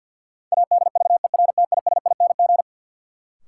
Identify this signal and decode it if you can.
Morse code, heard as a keyed tone, about 48 wpm, 700 Hz. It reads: ADVERTISING
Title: Sentences from Top 1000 Words - Review Part 6 - New Word Drill - 50wpm